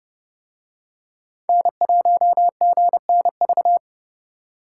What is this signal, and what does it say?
Morse code, heard as a keyed tone, about 28 wpm, 700 Hz. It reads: N1GNV